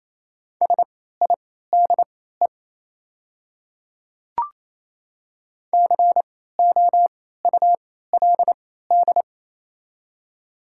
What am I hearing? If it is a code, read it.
Morse code, heard as a keyed tone, about 28 wpm, spaced as for 16 wpm, 700 Hz. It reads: SIDE COULD
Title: 3x Character Spacing - Top 200 Words - 28wpm